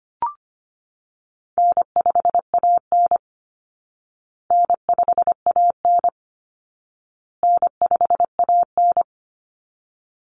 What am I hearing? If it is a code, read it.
Morse code, heard as a keyed tone, about 25 wpm, 700 Hz. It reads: N5AN N5AN N5AN
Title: Call Signs Repeated 2x - 25wpm